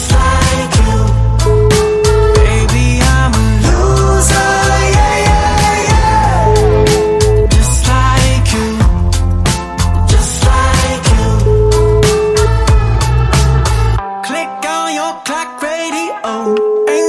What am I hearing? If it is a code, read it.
Ringback tone (Europe, 425 Hz), with noise.